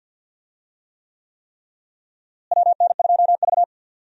Morse code, heard as a keyed tone, about 52 wpm, 700 Hz. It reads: WN2V
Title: USA Call Signs - 4 Characters - Repeated 3x - 50wpm